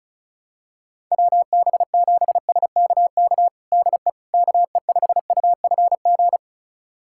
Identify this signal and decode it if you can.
Morse code, heard as a keyed tone, about 34 wpm, 700 Hz. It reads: WB7SKK DE KE5UFG
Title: USA Callsign DE Callsign - 6 Characters - 35wpm